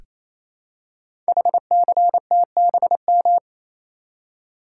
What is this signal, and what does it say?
Morse code, heard as a keyed tone, about 28 wpm, 700 Hz. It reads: HCTBM
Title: Code Groups - 5 Letters - 28wpm